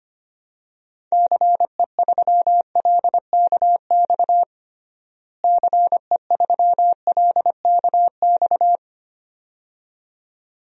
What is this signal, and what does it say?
Morse code, heard as a keyed tone, about 25 wpm, 700 Hz. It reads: CE3LKX CE3LKX
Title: DXCC-100 Call Signs - 6 Characters - Repeated 2x - 25wpm